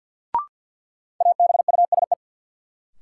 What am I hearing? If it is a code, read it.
Morse code, heard as a keyed tone, about 47 wpm, 700 Hz. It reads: ABUSE
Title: Sentences from Top 1300 Words - New Word Drill - 50wpm